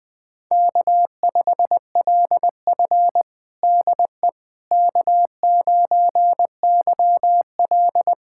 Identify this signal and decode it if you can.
Morse code, heard as a keyed tone, about 20 wpm, 700 Hz. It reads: K5LF DE K9YL
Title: USA Callsign DE Callsign - 4 Characters - 20wpm